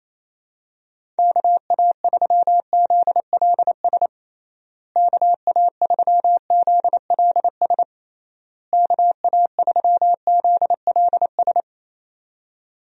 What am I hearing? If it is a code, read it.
Morse code, heard as a keyed tone, about 28 wpm, 700 Hz. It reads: KA3ZLS KA3ZLS KA3ZLS
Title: USA Call Signs - 6 Characters - Repeated 3x - 28wpm